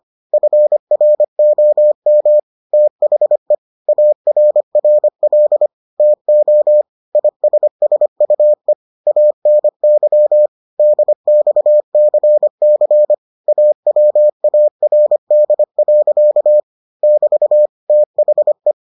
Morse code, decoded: FROM THE ARRL TO ISSUE ANY DXCC AWARD. <BT> THE

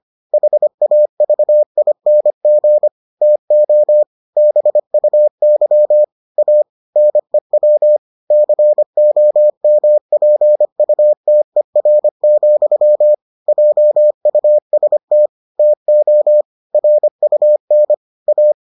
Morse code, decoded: HAVING TO BUY A NEW COMPUTER, JUST TO RUN A

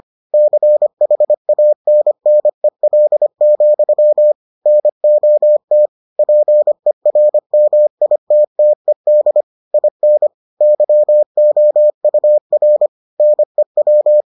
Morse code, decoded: CHANNEL, NOT PERMITTED IN YOUR NEW

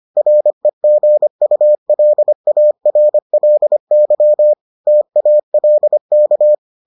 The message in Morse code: REGULARLY TALK

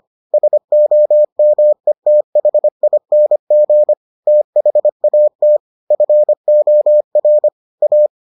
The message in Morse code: SOMETHING THAT FOR A

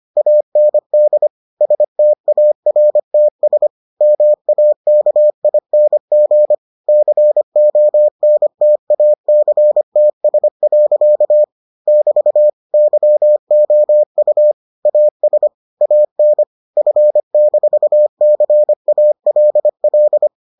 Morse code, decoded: AND STARTS MAKING CONTACTS. <BT> YOU AS AN F-CALL

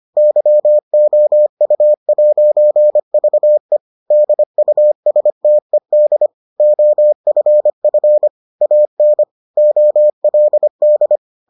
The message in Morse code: YOU'VE DUSTED OFF AN OLD

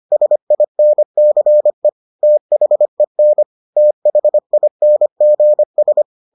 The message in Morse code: SINCE THEN THINGS

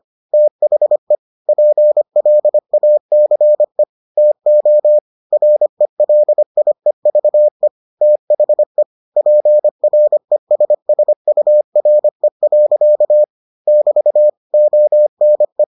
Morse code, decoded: THE PLACE TO RELIEVE THE PRESSURE. <BT> ONE

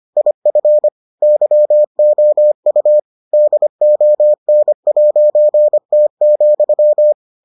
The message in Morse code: IF YOU DON'T,

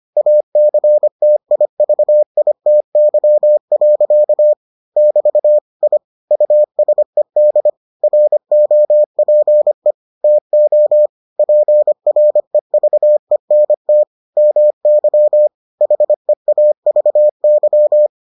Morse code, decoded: ACTIVITY. <BT> I USED ROPE TO PREVENT MY HEAVY

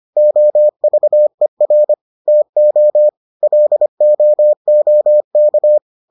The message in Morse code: OVER TO LOOK

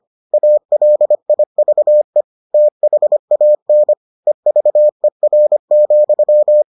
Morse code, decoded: ALIVE THAN EVER,

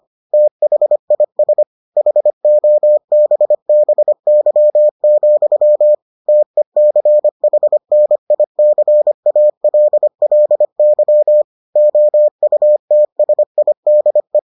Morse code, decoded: THIS HOBBY, TECHNICALLY OUTSIDE

